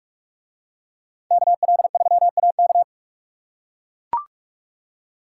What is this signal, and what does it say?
Morse code, heard as a keyed tone, about 44 wpm, 700 Hz. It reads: KL3AK